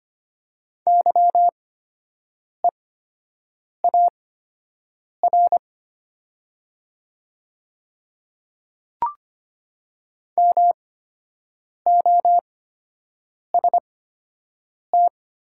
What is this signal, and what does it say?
Morse code, heard as a keyed tone, about 25 wpm, spaced as for 7 wpm, 700 Hz. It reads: YEAR MOST
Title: 8x Character Spacing - Top 300 Words - 25wpm